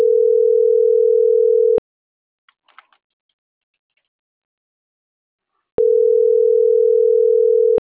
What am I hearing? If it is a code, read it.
Ringback tone (North America), 440 + 480 Hz.